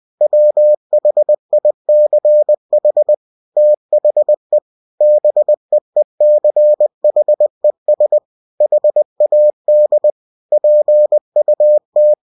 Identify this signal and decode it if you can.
Morse code, heard as a keyed tone, about 20 wpm, 600 Hz. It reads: WHICH THE BEECHES HAD PUT
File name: cw-books / winnie-the-pooh / winnie-20wpm-page-0028